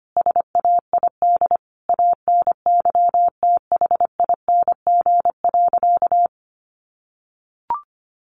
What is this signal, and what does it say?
Morse code, heard as a keyed tone, about 25 wpm, 700 Hz. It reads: SAID ANYTHING.
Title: Sentences from Top 1000 Words 25wpm